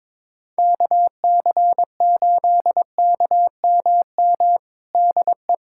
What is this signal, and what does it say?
Morse code, heard as a keyed tone, about 22 wpm, 700 Hz. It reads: KC8KMM DE
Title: USA Callsign DE Callsign 22wpm